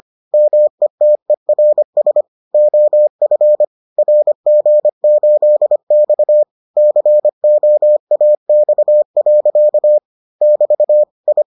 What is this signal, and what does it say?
Morse code, heard as a keyed tone, about 25 wpm, 600 Hz. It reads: METERS OF RG8X COAX. <BT> I